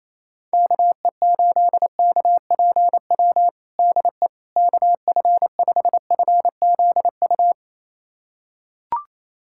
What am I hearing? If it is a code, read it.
Morse code, heard as a keyed tone, about 28 wpm, 700 Hz. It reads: KE8KPW DE KF5FZU
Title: USA Callsign DE Callsign 28wpm